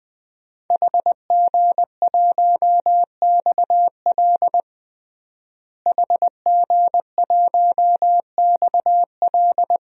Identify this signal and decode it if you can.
Morse code, heard as a keyed tone, about 20 wpm, 700 Hz. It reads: HG1XL HG1XL
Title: DXCC-100 Call Signs - 5 Characters - Repeated 2x - 20wpm